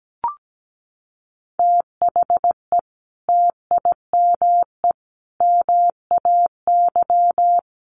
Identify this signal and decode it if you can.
Morse code, heard as a keyed tone, about 17 wpm, 700 Hz. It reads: THE TIME MAY
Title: Sentences from Top 1100 Words 17wpm